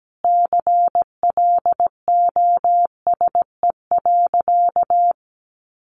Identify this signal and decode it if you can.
Morse code, heard as a keyed tone, about 17 wpm, 700 Hz. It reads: CLOSE.
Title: Sentences from Top 300 Words 17wpm